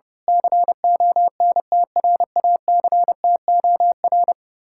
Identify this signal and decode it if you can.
Morse code, heard as a keyed tone, about 30 wpm, 700 Hz. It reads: CONTRACTOR
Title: Speed-Racing Common QSO Phrases - 30wpm